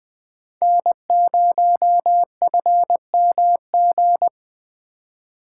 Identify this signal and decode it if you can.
Morse code, heard as a keyed tone, about 19 wpm, 700 Hz. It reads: N0FMG